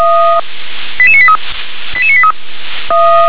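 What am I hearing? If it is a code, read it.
Five-tone selective call (ZVEI), heard as quick runs of five tones, tones of about 70 ms. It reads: 88983 88983
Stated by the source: ZVEI five-tone selective call received by SDR.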